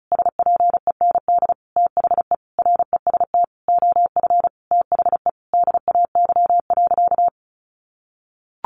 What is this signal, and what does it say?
Morse code, heard as a keyed tone, about 34 wpm, 700 Hz. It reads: SPEND THE REST OF THE DAY.
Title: Sentences from Top 1400 Words 35wpm